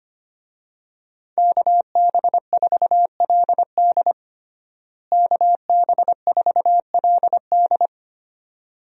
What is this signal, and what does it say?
Morse code, heard as a keyed tone, about 25 wpm, 700 Hz. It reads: KB4LD KB4LD
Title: USA Call Signs - 5 Characters - Repeated 2x - 25wpm